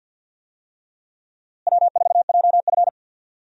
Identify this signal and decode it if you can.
Morse code, heard as a keyed tone, about 47 wpm, 700 Hz. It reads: W4JF
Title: USA Call Signs - 4 Characters - Repeated 3x - 50wpm